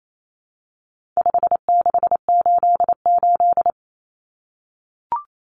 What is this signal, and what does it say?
Morse code, heard as a keyed tone, about 28 wpm, 700 Hz. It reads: 5688
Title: Common Cut Numbers - 4-Digit 28wpm